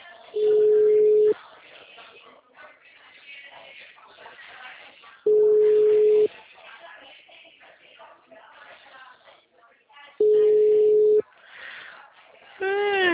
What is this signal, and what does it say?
Ringback tone (Europe, 425 Hz), with speech.